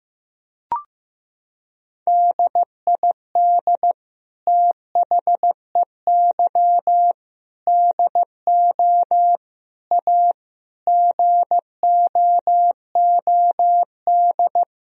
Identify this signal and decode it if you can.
Morse code, heard as a keyed tone, about 15 wpm, 700 Hz. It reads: DID THEY DO A GOOD
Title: Sentences from Top 1000 Words - Review Part 3 - 15wpm